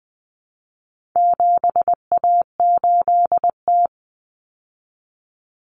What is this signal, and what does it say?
Morse code, heard as a keyed tone, about 20 wpm, 700 Hz. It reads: 7A8T
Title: Common Cut Numbers - 4-Digit 20wpm